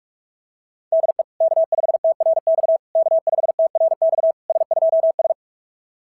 Morse code, heard as a keyed tone, about 47 wpm, 660 Hz. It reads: DE K5TRX K5TRX S2S
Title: SOTA QSO Practice 45wpm